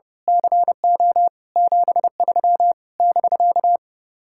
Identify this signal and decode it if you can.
Morse code, heard as a keyed tone, about 28 wpm, 700 Hz. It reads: CO 73 <BK>
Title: Speed-Racing Common QSO Phrases - 30wpm